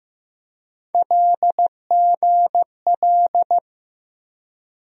Morse code, heard as a keyed tone, about 15 wpm, 700 Hz. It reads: LGL
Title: USA Call Sign Postfixes - 3 Letters 15wpm